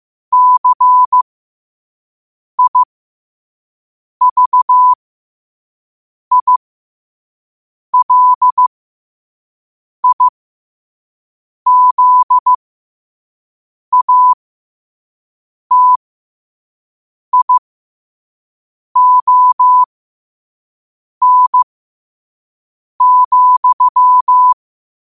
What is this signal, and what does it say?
Morse code, heard as a keyed tone, about 15 wpm, spaced as for 5 wpm, 1000 Hz. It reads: CIVILIZATION,
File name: qotdfiles / 5wpm